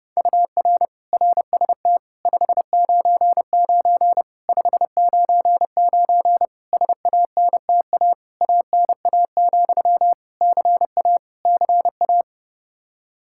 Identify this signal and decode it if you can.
Morse code, heard as a keyed tone, about 30 wpm, 700 Hz. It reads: UR RST 599 599 SANTA ANA, CA CA
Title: Practice QSO - Exchange 1 - Beginning - 30wpm